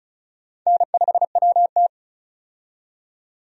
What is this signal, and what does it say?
Morse code, heard as a keyed tone, about 36 wpm, 700 Hz. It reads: N5WT
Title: USA Call Signs - 4 Characters - 35wpm